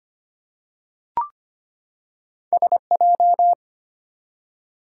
Morse code, heard as a keyed tone, about 25 wpm, 700 Hz. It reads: SJ
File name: Two Letter-Number Combinations - 25wpm